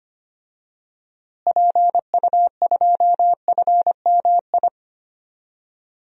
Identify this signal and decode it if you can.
Morse code, heard as a keyed tone, about 25 wpm, 700 Hz. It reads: PU2FMI
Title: DXCC-100 Call Signs - 6 Characters - Repeated 3x - 25wpm